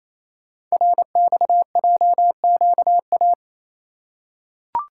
Morse code, heard as a keyed tone, about 27 wpm, 690 Hz. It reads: RXJQA
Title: Code Groups - 5 Letters - 28wpm